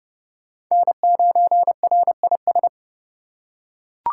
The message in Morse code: N9RIS

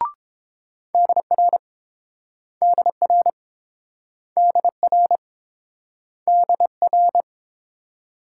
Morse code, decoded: DR DR DR DR

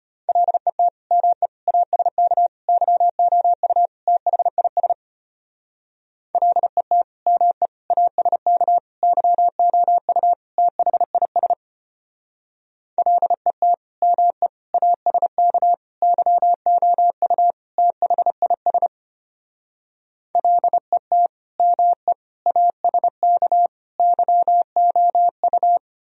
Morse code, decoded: LET ME ASK YOU THIS LET ME ASK YOU THIS LET ME ASK YOU THIS LET ME ASK YOU

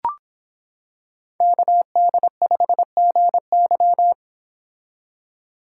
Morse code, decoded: KD5GY

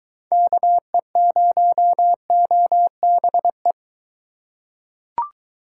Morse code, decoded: KE0OBE

